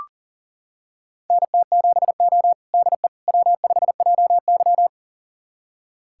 Morse code, heard as a keyed tone, about 40 wpm, 700 Hz. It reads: NT7O DE W5JY
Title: USA Callsign DE Callsign - 4 Characters - 40wpm